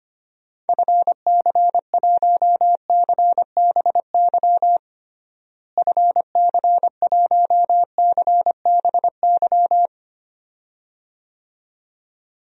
Morse code, heard as a keyed tone, about 25 wpm, 700 Hz. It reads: FC1CBY FC1CBY
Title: DXCC-100 Call Signs - 6 Characters - Repeated 2x - 25wpm